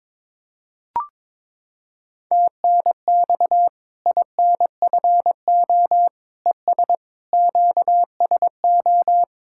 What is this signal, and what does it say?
Morse code, heard as a keyed tone, about 22 wpm, 700 Hz. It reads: TNX INFO ES QSO
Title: Practice QSO - Exchange 4 - Ending - 22wpm